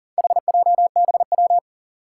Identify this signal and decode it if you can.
Morse code, heard as a keyed tone, about 40 wpm, 700 Hz. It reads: HJBW
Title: Four Challenging Letter-Number Combinations 40wpm